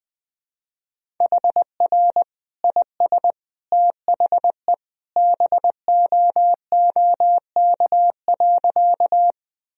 Morse code, heard as a keyed tone, about 20 wpm, 700 Hz. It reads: HR IS THE BOOK.
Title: Sentences from Top 300 Words and Abbreviations 20wpm